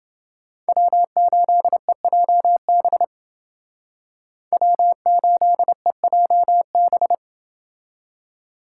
Morse code, heard as a keyed tone, about 28 wpm, 700 Hz. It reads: W8EJB W8EJB